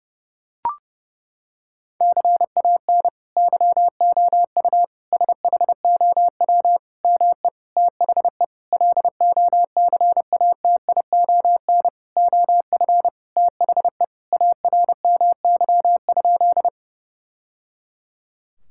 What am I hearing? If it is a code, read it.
Morse code, heard as a keyed tone, about 30 wpm, 700 Hz. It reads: CAN YOU SHOW ME THE LOCATION OF THE ARMY?
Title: Sentences from Top 1000 Words - Review Part 6 - 30wpm